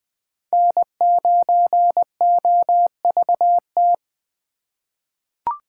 Morse code, heard as a keyed tone, about 20 wpm, 700 Hz. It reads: N9OVT
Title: USA Call Signs - 5 Characters - 2x Character Spacing - 20wpm